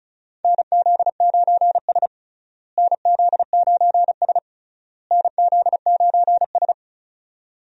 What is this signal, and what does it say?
Morse code, heard as a keyed tone, about 36 wpm, 700 Hz. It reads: NZ9S NZ9S NZ9S
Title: USA Call Signs - 4 Characters - Repeated 3x - 35wpm